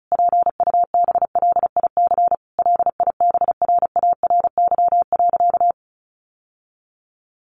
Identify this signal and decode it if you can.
Morse code, heard as a keyed tone, about 34 wpm, 700 Hz. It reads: PUBLIC LIBRARY.
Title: Sentences from Top 1300 Words 35wpm